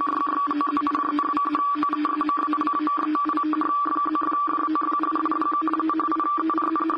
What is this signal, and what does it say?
Two-tone teleprinter signal, about 850 Hz shift.